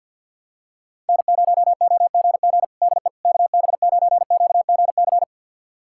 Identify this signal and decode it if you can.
Morse code, heard as a keyed tone, about 52 wpm, 700 Hz. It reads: N0OGG DE KB9QGC
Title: USA Callsign DE Callsign 50wpm